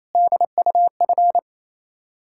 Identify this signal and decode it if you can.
Morse code, heard as a keyed tone, about 28 wpm, 700 Hz. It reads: DUF
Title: Ontario QSO Party Exchange 28wpm